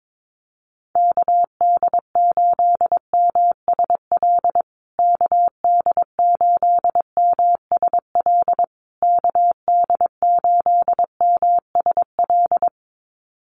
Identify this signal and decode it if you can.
Morse code, heard as a keyed tone, about 22 wpm, 700 Hz. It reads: KD8MSL KD8MSL KD8MSL
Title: USA Call Signs Repeated 3x 22wpm